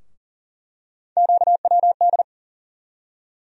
Morse code, heard as a keyed tone, about 40 wpm, 700 Hz. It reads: QWD
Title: USA Call Sign Postfixes - 3 Letters 40wpm